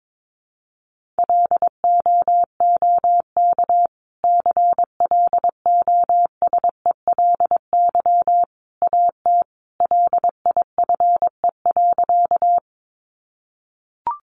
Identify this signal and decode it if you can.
Morse code, heard as a keyed tone, about 22 wpm, 700 Hz. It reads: LOOK CLOSELY AT LIFE.
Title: Sentences from Top 1400 Words 22wpm